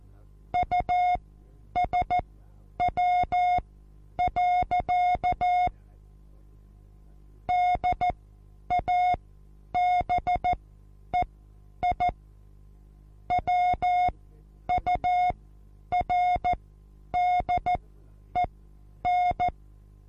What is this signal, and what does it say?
Morse code, heard as a keyed tone, about 14 wpm, spaced as for 9 wpm, 720 Hz. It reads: USW. DABEI WURDEN
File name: Onlinelehrgang / morsen / DJ4UF Morsen23